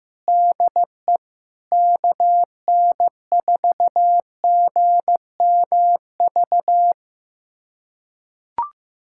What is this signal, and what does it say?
Morse code, heard as a keyed tone, about 16 wpm, 700 Hz. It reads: DE KN4GMV
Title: USA Callsign DE Callsign 15wpm